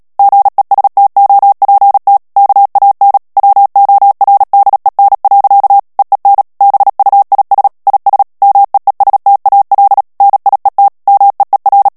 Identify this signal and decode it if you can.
Morse code, heard as a keyed tone, about 37 wpm, 800 Hz. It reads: GESTOPT KAN WORDEN. EEN BUIS IS MEESTAL NIET MEER
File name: morse mp3 / HOLLANDAIS texte clair 2 vitesse 35 mots minute